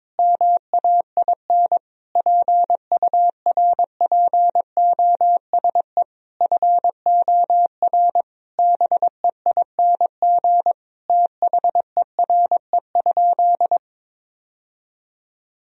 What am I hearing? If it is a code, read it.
Morse code, heard as a keyed tone, about 22 wpm, 700 Hz. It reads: MAIN PURPOSE FOR BEING THERE?
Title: Sentences from Top 1000 Words - Review Part 2 - 22wpm